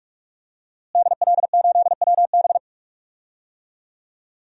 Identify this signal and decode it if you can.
Morse code, heard as a keyed tone, about 46 wpm, 700 Hz. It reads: DL8WB
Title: DXCC-100 Call Signs - 5 Characters - Repeated 2x - 45wpm